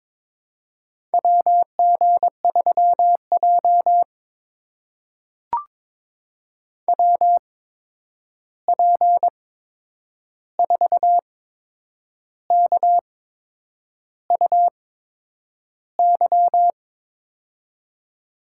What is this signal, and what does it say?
Morse code, heard as a keyed tone, about 22 wpm, spaced as for 8 wpm, 700 Hz. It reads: WG3J WP4KUY
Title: USA Call Signs - 8x Character Spacing - 22wpm